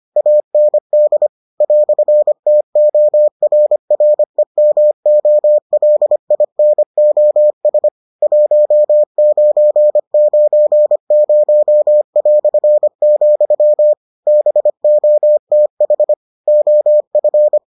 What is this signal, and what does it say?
Morse code, heard as a keyed tone, about 25 wpm, 600 Hz. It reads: AND "TORREMOLINOS 1990", BOTH OF